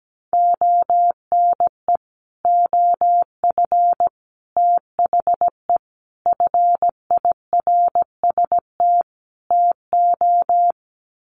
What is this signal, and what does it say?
Morse code, heard as a keyed tone, about 17 wpm, 700 Hz. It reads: ONE OF THE FIRST TO
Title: Sets of 5 Words Encore 17wpm